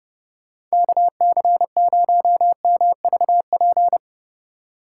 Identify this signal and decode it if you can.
Morse code, heard as a keyed tone, about 38 wpm, 700 Hz. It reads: KC0MVP
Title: Speed Racing - USA Call Signs 30wpm